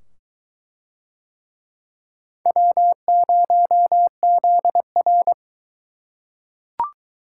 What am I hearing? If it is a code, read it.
Morse code, heard as a keyed tone, about 22 wpm, 700 Hz. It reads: W0ZR